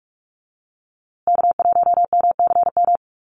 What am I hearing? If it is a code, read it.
Morse code, heard as a keyed tone, about 44 wpm, 700 Hz. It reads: K1MCM
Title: USA Call Signs Repeated 3x 45wpm